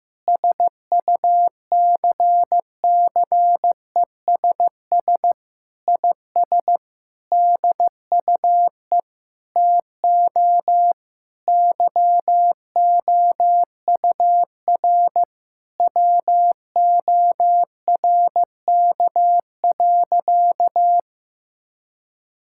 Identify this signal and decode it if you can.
Morse code, heard as a keyed tone, about 15 wpm, 700 Hz. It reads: SUCCESS IS DUE TO YOUR WORK.
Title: Sentences from Top 1000 Words - Review Part 6 - 15wpm